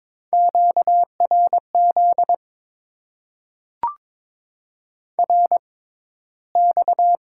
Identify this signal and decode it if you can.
Morse code, heard as a keyed tone, about 22 wpm, spaced as for 14 wpm, 700 Hz. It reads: QRZ RX
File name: Top 90 QSO Elements - 6x Character Spacing - 22wpm